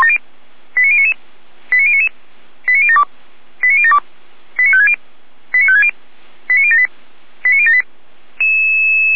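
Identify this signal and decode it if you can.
Five-tone selective call (ZVEI), heard as quick runs of five tones, tones of about 70 ms. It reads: 89099 89099 89072 89072 89560 89560 89078 89078
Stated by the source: ZVEI five-tone selective call received on SDR.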